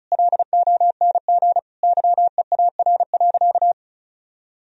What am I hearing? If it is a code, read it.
Morse code, heard as a keyed tone, about 34 wpm, 700 Hz. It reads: LONG YEAR.
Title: Sentences from Top 1000 Words - Review Part 1 - 35wpm